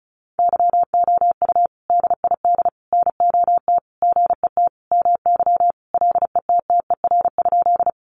Morse code, heard as a keyed tone, about 34 wpm, 700 Hz. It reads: YOU DID NOT GET MY LETTER?
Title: Sentences from Top 600 Words 35wpm